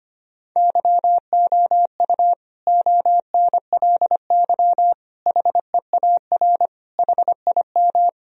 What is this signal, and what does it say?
Morse code, heard as a keyed tone, about 25 wpm, 700 Hz. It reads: YOU ONLY HEAR HIM
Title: Sentences from Top 1000 Words - Review Part 2 - 25wpm